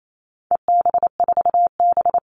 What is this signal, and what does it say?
Morse code, heard as a keyed tone, about 28 wpm, 700 Hz. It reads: EB4B
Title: All Cut Numbers - 4-Digit 28wpm